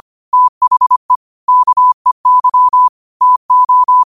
Morse code, heard as a keyed tone, about 25 wpm, 1000 Hz. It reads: THE KEY TO
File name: qotdfiles / 25wpm